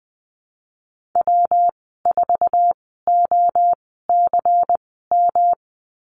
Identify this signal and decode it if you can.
Morse code, heard as a keyed tone, about 20 wpm, spaced as for 15 wpm, 700 Hz. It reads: W4OCM
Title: 186 Lesson 30g - Letter b - Call Signs 20wpm